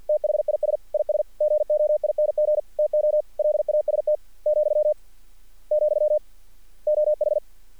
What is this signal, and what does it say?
Morse code, heard as a keyed tone, about 50 wpm, 600 Hz. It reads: THIS IS GOING TO LAST , , OH
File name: archived / CRSnet